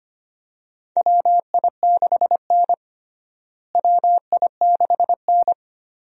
Morse code, heard as a keyed tone, about 25 wpm, 700 Hz. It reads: WI6N WI6N
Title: USA Call Signs - 4 Characters - Repeated 3x - 25wpm